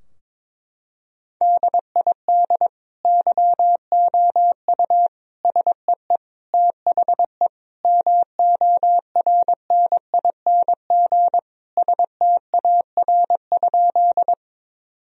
Morse code, heard as a keyed tone, about 22 wpm, 700 Hz. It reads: DID YOU SEE THE MORNING STAR?
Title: Sentences from Top 1000 Words - Review Part 2 - 22wpm